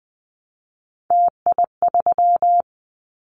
Morse code, heard as a keyed tone, about 20 wpm, 700 Hz. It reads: TI3
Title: Callsign Prefixes for Top 100 DXCC - 20wpm